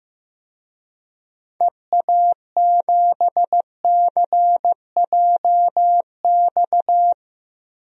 Morse code, heard as a keyed tone, about 15 wpm, 700 Hz. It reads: EA7CJX
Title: DXCC-100 Call Signs - 6 Characters - Repeated 3x - 15wpm